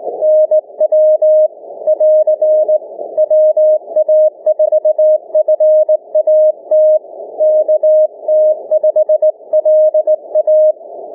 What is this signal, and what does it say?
Morse code, heard as a keyed tone, about 20 wpm, 620 Hz. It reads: NW <AR> WA4FAT KT5LA
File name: cw / ssn 30M